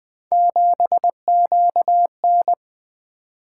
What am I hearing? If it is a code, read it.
Morse code, heard as a keyed tone, about 24 wpm, 700 Hz. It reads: 7QN
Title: Speed-Racing - Three Letter-Number Combinations - 20wpm